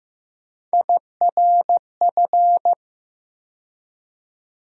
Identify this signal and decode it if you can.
Morse code, heard as a keyed tone, about 15 wpm, 700 Hz. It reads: IRF
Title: USA Call Sign Postfixes - 3 Letters 15wpm